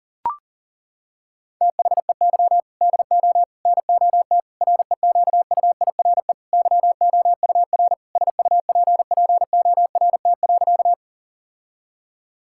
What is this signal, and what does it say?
Morse code, heard as a keyed tone, about 40 wpm, 690 Hz. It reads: THEY DO NOT REQUIRE YOUR SUPPORT.
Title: Sentences from Top 1000 Words - Review Part 6 - 40wpm